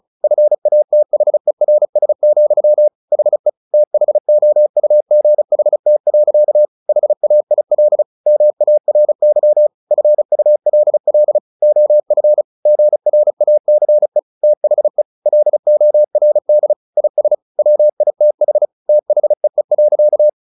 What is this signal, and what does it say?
Morse code, heard as a keyed tone, about 34 wpm, 600 Hz. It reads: FATHERS, HE THOUGHT. HAIL MARY FULL OF GRACE THE LORD IS WITH THEE.
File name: the old man and the sea / the old man and the sea 1 35wpm